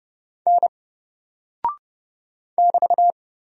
Morse code, heard as a keyed tone, about 30 wpm, 700 Hz. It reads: N <BT>
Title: Single Character (Letter:Number:Punctuation:Prosign) Alt 30wpm